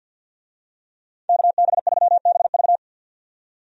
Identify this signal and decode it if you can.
Morse code, heard as a keyed tone, about 50 wpm, 700 Hz. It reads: KB3BV
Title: USA Call Signs - 5 Characters - 50wpm